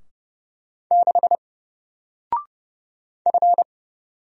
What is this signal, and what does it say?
Morse code, heard as a keyed tone, about 31 wpm, 700 Hz. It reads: 6 F
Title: Single Letter-Number 30wpm